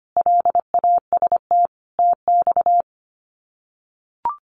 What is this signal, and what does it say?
Morse code, heard as a keyed tone, about 25 wpm, 700 Hz. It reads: LAST TX
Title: Common QSO Phrases 25wpm